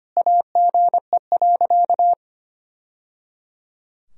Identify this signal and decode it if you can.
Morse code, heard as a keyed tone, about 25 wpm, 700 Hz. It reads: AGE.
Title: Sentences from Top 1000 Words - Review Part 3 - 25wpm